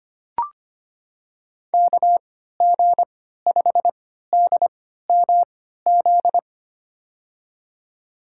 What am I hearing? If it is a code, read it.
Morse code, heard as a keyed tone, about 25 wpm, spaced as for 14 wpm, 700 Hz. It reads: KG5DMZ
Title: USA Call Signs - 6 Characters - 3x Character Spacing - 25wpm